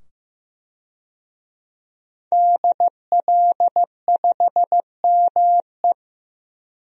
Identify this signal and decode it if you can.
Morse code, heard as a keyed tone, about 15 wpm, 700 Hz. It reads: DL5ME
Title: DXCC-100 Call Signs - 5 Characters - 15wpm